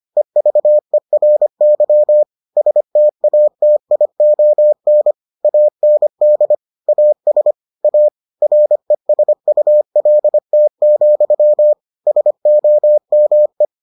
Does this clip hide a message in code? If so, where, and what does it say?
Yes: Morse code from 0.2 to 13.7 s, heard as a keyed tone, about 25 wpm, 600 Hz. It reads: EVERY STATION AND AS A RESULT, SOME